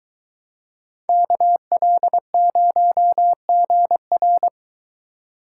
Morse code, heard as a keyed tone, about 22 wpm, 700 Hz. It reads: KL0GR